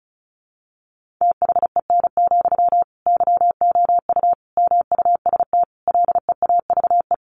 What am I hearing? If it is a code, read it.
Morse code, heard as a keyed tone, about 34 wpm, 700 Hz. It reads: THEN, YOU MUST LEAVE
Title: Sentences from Top 1500 Words 35wpm